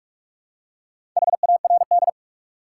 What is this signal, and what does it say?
Morse code, heard as a keyed tone, about 47 wpm, 700 Hz. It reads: HARD
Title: Four Letter Words - 45wpm